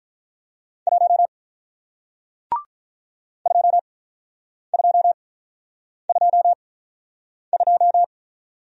Morse code, heard as a keyed tone, about 44 wpm, spaced as for 9 wpm, 700 Hz. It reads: 1 2222
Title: Speed-Racing Single Letter-Number 35wpm